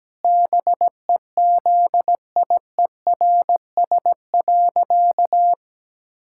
Morse code, heard as a keyed tone, about 17 wpm, 700 Hz. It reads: BEZIERS.
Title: Pangram Challenge v2 - 17wpm